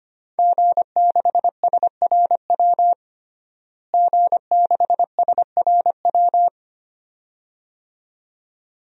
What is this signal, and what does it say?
Morse code, heard as a keyed tone, about 25 wpm, 700 Hz. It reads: G6SRW G6SRW
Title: DXCC-100 Call Signs - 5 Characters - Repeated 2x - 25wpm